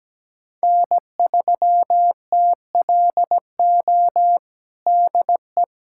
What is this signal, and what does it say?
Morse code, heard as a keyed tone, about 17 wpm, 700 Hz. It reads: N3TLO DE
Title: USA Callsign DE Callsign - 5 Characters - 17wpm